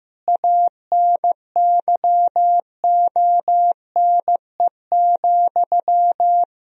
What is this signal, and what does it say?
Morse code, heard as a keyed tone, about 15 wpm, 700 Hz. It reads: ANYONE,
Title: Sentences from Top 1000 Words - Review Part 1 - 15wpm